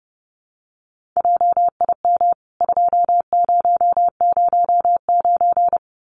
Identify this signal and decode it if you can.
Morse code, heard as a keyed tone, about 30 wpm, 700 Hz. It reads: JIM 2009
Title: CWT Exchange 30wpm